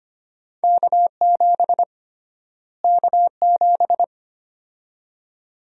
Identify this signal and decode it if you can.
Morse code, heard as a keyed tone, about 25 wpm, 700 Hz. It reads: K7 K7
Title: USA Call Sign Prefixes - Repeated 2x - 25wpm